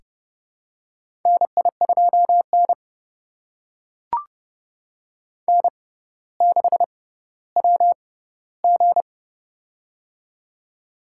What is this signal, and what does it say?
Morse code, heard as a keyed tone, about 30 wpm, spaced as for 15 wpm, 700 Hz. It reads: NI2N N6WG
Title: USA Call Signs - 4 Characters - 6x Character Spacing - 30wpm